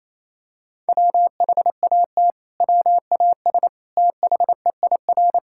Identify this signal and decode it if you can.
Morse code, heard as a keyed tone, about 28 wpm, 700 Hz. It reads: WHAT WAS THEIR
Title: Sentences from Top 1000 Words - Review Part 6 - 28wpm